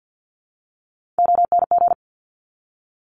Morse code, heard as a keyed tone, about 50 wpm, 700 Hz. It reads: ONG
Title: Top Three Letter Combinations in Words 50wpm